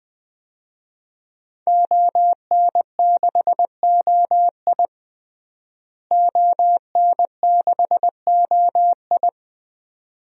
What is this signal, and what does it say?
Morse code, heard as a keyed tone, about 20 wpm, 700 Hz. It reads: ON6OI ON6OI
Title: DXCC-100 Call Signs - 5 Characters - Repeated 2x - 20wpm